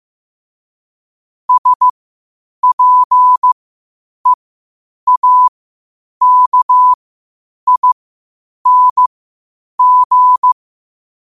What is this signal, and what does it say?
Morse code, heard as a keyed tone, about 15 wpm, spaced as for 8 wpm, 1000 Hz. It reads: SPEAKING
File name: qotdfiles / 8wpm